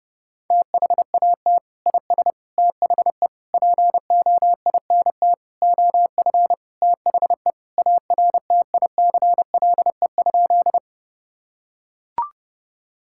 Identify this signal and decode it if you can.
Morse code, heard as a keyed tone, about 30 wpm, 700 Hz. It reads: THAT IS THE POINT OF THE ARTICLE?
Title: Sentences from Top 1000 Words - Review Part 5 - 30wpm